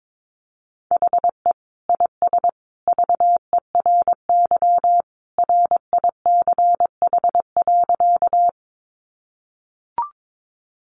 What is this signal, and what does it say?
Morse code, heard as a keyed tone, about 22 wpm, 700 Hz. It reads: HE IS VERY RICH.
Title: Sentences from Top 1100 Words 22wpm